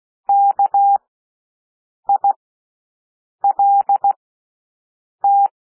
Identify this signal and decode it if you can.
Morse code, heard as a keyed tone, about 16 wpm, spaced as for 6 wpm, 800 Hz. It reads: KILT
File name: cwmp3 / W1 SOUND WORDS 16 wpm